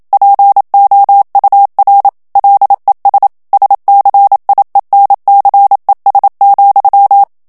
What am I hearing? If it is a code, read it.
Morse code, heard as a keyed tone, about 27 wpm, 800 Hz. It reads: POUR LES SCIENCES,
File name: CW / FICHIERS-MP3 / FRANCAIS TEXTE CLAIR 1 VITESSE 26 MOTS MINUTES